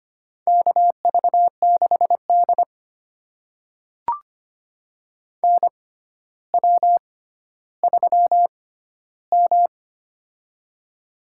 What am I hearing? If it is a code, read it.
Morse code, heard as a keyed tone, about 25 wpm, spaced as for 13 wpm, 700 Hz. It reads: KV6D NW3M
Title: USA Call Signs - 4 Characters - 6x Character Spacing - 25wpm